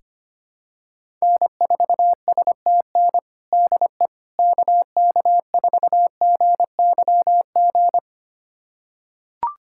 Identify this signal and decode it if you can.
Morse code, heard as a keyed tone, about 25 wpm, 700 Hz. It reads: N4STN DE KK4GYG
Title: USA Callsign DE Callsign 25wpm